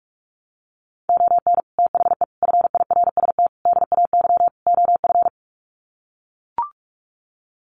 Morse code, heard as a keyed tone, about 44 wpm, 700 Hz. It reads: ON THE FIRST DAY OF
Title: Sets of 5 Words Encore 45wpm